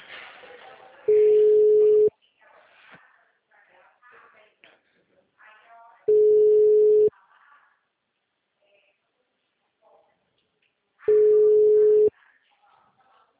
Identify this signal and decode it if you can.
Ringback tone (Europe, 425 Hz), with speech.
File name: out- -2014- - - .51166